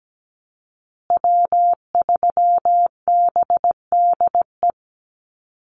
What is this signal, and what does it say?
Morse code, heard as a keyed tone, about 17 wpm, 700 Hz. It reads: W3BDE
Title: USA Call Signs Repeated 2x 17wpm